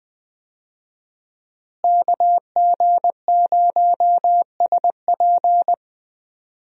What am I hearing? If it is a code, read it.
Morse code, heard as a keyed tone, about 20 wpm, 700 Hz. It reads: KG0SP
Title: USA Call Signs - 5 Characters - Repeated 3x - 20wpm